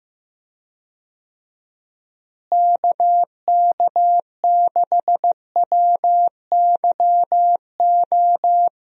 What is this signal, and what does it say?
Morse code, heard as a keyed tone, about 15 wpm, 700 Hz. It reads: KK6WYO
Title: USA Call Signs - 6 Characters - Repeated 2x - 15wpm